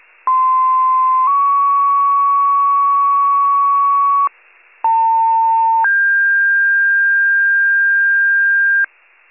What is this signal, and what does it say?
Two-tone page, heard as a one-second tone, then a three-second tone, about 1040 then 1120 Hz, about 900 then 1600 Hz. It is heard from 0.3 to 8.8 s.